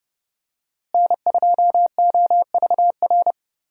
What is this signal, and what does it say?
Morse code, heard as a keyed tone, about 30 wpm, 700 Hz. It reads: N2OVR
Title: USA Call Signs - 5 Characters - Repeated 3x - 30wpm